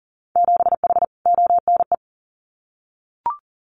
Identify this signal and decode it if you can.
Morse code, heard as a keyed tone, about 37 wpm, 700 Hz. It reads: 7H ONE
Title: Winter Field Day Exchanges Encore 40wpm